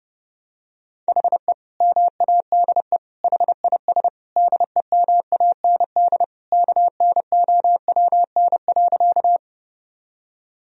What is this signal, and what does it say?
Morse code, heard as a keyed tone, about 30 wpm, 700 Hz. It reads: HE MADE HIS DEMAND KNOWN.
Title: Sentences from Top 1000 Words - Review Part 4 - 30wpm